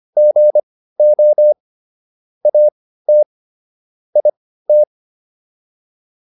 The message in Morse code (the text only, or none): GO AT IT